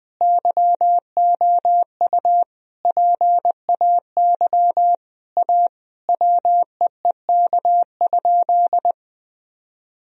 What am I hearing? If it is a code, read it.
Morse code, heard as a keyed tone, about 21 wpm, 700 Hz. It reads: YOU PAY A WEEK?
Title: Sentences from Top 1000 Words - Review Part 2 - 20wpm